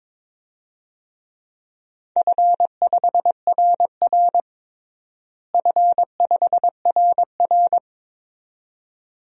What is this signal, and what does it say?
Morse code, heard as a keyed tone, about 22 wpm, 700 Hz. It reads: F5RR F5RR
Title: DXCC-100 Call Signs - 4 Characters - Repeated 2x - 22wpm